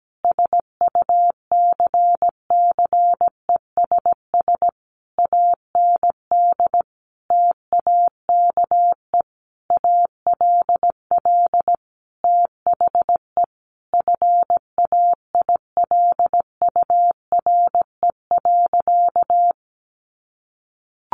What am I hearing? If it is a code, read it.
Morse code, heard as a keyed tone, about 17 wpm, 700 Hz. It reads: SUCCESS AND TAKE ALL THE FAILURE.
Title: Sentences from Top 900 Words 17wpm